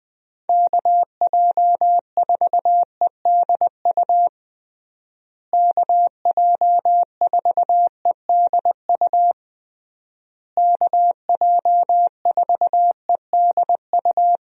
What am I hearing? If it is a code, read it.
Morse code, heard as a keyed tone, about 20 wpm, 700 Hz. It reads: KJ4EDU KJ4EDU KJ4EDU
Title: USA Call Signs - 6 Characters - Repeated 3x - 20wpm